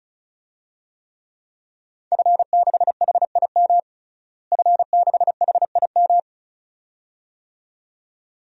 Morse code, heard as a keyed tone, about 34 wpm, 700 Hz. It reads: F6HIM F6HIM
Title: DXCC-100 Call Signs - 5 Characters - Repeated 2x - 35wpm